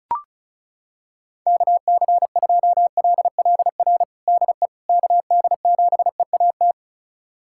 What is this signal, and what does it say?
Morse code, heard as a keyed tone, about 34 wpm, 690 Hz. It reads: KC2LLR DE KD7EAT
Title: USA Callsign DE Callsign - 6 Characters - 35wpm